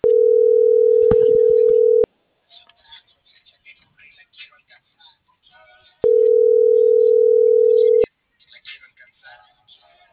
Ringback tone (North America), 440 + 480 Hz.